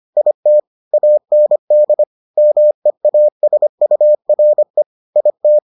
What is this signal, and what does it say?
Morse code, heard as a keyed tone, about 25 wpm, 600 Hz. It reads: IT AND MEASURE IT